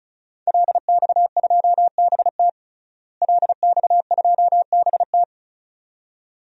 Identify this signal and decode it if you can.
Morse code, heard as a keyed tone, about 34 wpm, 700 Hz. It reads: LX2BT LX2BT
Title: DXCC-100 Call Signs - 5 Characters - Repeated 2x - 35wpm